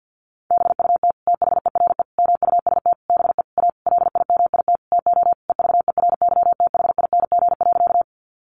Morse code, heard as a keyed tone, about 50 wpm, 700 Hz. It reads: BUT THERE MUST BE A LIMIT TO EVERYTHING.
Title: Sentences from Top 1500 Words 50wpm